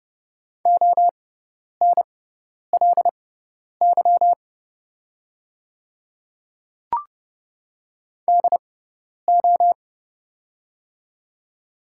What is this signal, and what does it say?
Morse code, heard as a keyed tone, about 31 wpm, spaced as for 10 wpm, 690 Hz. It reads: ONLY DO
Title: 6x Character Spacing - Top 200 Words - 30wpm